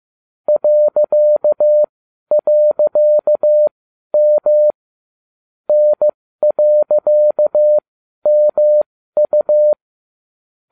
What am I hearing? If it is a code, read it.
Morse code, heard as a keyed tone, about 15 wpm, spaced as for 12 wpm, 600 Hz. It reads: ..M N.MU